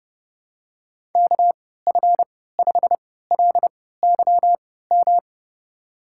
Morse code, heard as a keyed tone, about 30 wpm, spaced as for 17 wpm, 700 Hz. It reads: KF5LYM
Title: USA Call Signs - 6 Characters - 3x Character Spacing - 30wpm